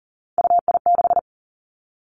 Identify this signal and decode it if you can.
Morse code, heard as a keyed tone, about 38 wpm, 690 Hz. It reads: UI6
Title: Three Challenging Letter-Number Combinations 40wpm